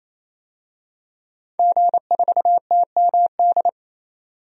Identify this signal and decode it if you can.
Morse code, heard as a keyed tone, about 28 wpm, 700 Hz. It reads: G4TMD
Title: DXCC-100 Call Signs - 5 Characters - 28wpm